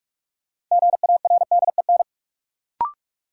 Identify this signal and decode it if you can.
Morse code, heard as a keyed tone, about 46 wpm, 700 Hz. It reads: GARDEN
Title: Sentences from Top 700 Words - New Word Drill - 45wpm